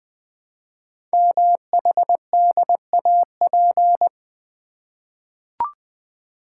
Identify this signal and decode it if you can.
Morse code, heard as a keyed tone, about 21 wpm, 700 Hz. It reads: MHDAP
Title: Code Groups - 5 Letters - 20wpm